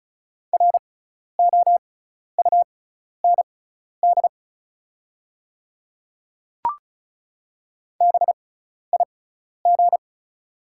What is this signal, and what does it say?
Morse code, heard as a keyed tone, about 36 wpm, spaced as for 12 wpm, 690 Hz. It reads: ROUND BIG
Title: 6x Character Spacing - Top 200 Words - 35wpm